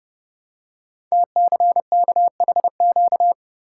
Morse code, heard as a keyed tone, about 28 wpm, 700 Hz. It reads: TCKHQ
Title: Code Groups - 5 Letters - 30wpm